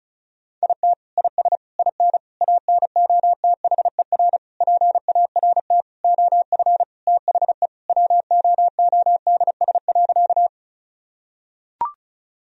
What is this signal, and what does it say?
Morse code, heard as a keyed tone, about 36 wpm, 700 Hz. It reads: IT IS IN ANOTHER PART OF THE WOODS.
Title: Sentences from Top 1000 Words - Review Part 2 - 35wpm